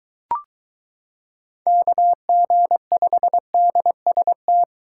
Morse code, heard as a keyed tone, about 23 wpm, 700 Hz. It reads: KG5DST